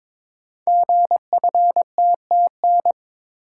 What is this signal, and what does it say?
Morse code, heard as a keyed tone, about 23 wpm, 700 Hz. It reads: GFTTN
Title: Code Groups - 5 Letters - 22wpm